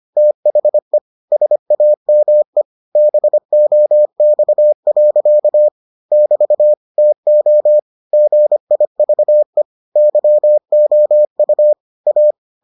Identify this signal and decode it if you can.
Morse code, heard as a keyed tone, about 25 wpm, 600 Hz. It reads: THE SAME BOX. <BT> TO GIVE YOU A